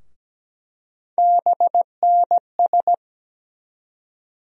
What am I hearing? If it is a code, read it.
Morse code, heard as a keyed tone, about 17 wpm, 700 Hz. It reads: BNS
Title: USA Call Sign Postfixes - 3 Letters 17wpm